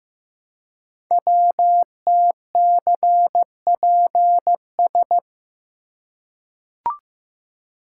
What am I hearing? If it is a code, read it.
Morse code, heard as a keyed tone, about 15 wpm, 700 Hz. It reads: WTCPS
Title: Code Groups - 5 Letters - 15wpm